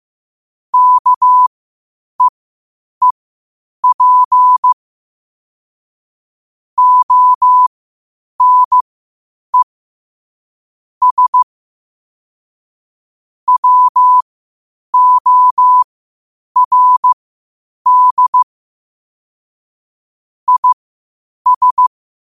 Morse code, heard as a keyed tone, about 15 wpm, spaced as for 8 wpm, 1000 Hz. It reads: KEEP ONE S WORD IS
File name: qotdfiles / 8wpm